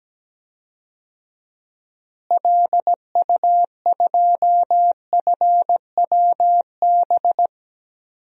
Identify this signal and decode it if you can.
Morse code, heard as a keyed tone, about 17 wpm, 700 Hz. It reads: LU2FWB
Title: DXCC-100 Call Signs - 6 Characters - Repeated 2x - 17wpm